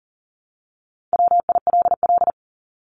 Morse code, heard as a keyed tone, about 40 wpm, 700 Hz. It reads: WILL
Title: Top 200 Words Repeated 40wpm